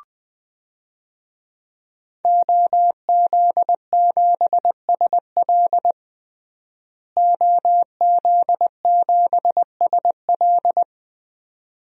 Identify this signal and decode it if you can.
Morse code, heard as a keyed tone, about 20 wpm, 700 Hz. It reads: OZ7SL OZ7SL
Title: DXCC-100 Call Signs - 5 Characters - Repeated 2x - 20wpm